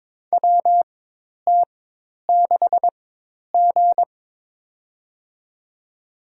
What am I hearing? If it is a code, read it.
Morse code, heard as a keyed tone, about 22 wpm, spaced as for 10 wpm, 700 Hz. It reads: WT6G
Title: USA Call Signs - 4 Characters - 4x Character Spacing - 22wpm